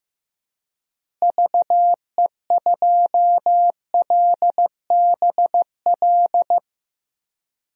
Morse code, heard as a keyed tone, about 15 wpm, 700 Hz. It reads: VE2LBL
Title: DXCC-100 Call Signs - 6 Characters - Repeated 3x - 15wpm